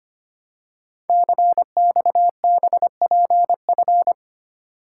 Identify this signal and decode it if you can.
Morse code, heard as a keyed tone, about 25 wpm, 700 Hz. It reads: CXBPF
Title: Code Groups - 5 Letters - 25wpm